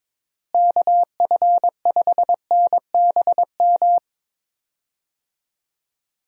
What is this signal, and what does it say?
Morse code, heard as a keyed tone, about 22 wpm, 700 Hz. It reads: KF5NBM
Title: USA Call Signs - 6 Characters - Repeated 2x - 22wpm